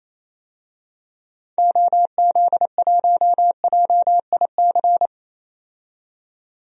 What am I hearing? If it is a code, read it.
Morse code, heard as a keyed tone, about 27 wpm, 700 Hz. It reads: OZ1JIC
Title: DXCC-100 Call Signs - 6 Characters - 28wpm